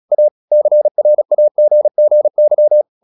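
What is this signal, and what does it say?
Morse code, heard as a keyed tone, about 37 wpm, 600 Hz. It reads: A CRAGGY
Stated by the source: Morse practice at 36 wpm, text of Ken Follett's novel Never, from CW med Gnister.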